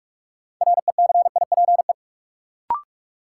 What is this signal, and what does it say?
Morse code, heard as a keyed tone, about 44 wpm, 700 Hz. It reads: RECIPE
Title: Sentences from Top 1200 Words - New Word Drill - 45wpm